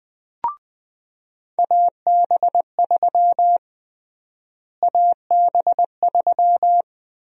Morse code, heard as a keyed tone, about 20 wpm, 700 Hz. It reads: AB3 AB3
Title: USA Call Sign Prefixes - Repeated 2x - 20wpm